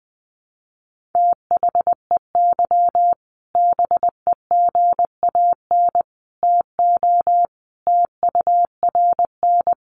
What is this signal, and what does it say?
Morse code, heard as a keyed tone, about 20 wpm, 700 Hz. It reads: THEY BEGAN TO TURN
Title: Sentences from Top 1100 Words 20wpm